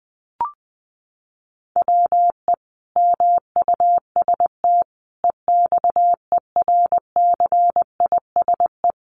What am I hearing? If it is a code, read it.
Morse code, heard as a keyed tone, about 20 wpm, 700 Hz. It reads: WE MUST EXERCISE
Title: Sentences from Top 1200 Words 20wpm